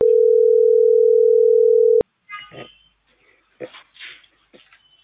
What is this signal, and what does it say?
Ringback tone (North America), 440 + 480 Hz.